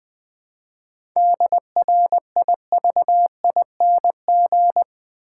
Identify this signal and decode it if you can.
Morse code, heard as a keyed tone, about 20 wpm, 700 Hz. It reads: DRIVING
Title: 7 Letter Words - First 3 Letters Given - 20wpm